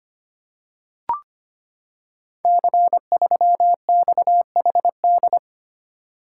Morse code, heard as a keyed tone, about 25 wpm, 700 Hz. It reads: C3XHD
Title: Code Groups - 5 Letter:Numbers - 25wpm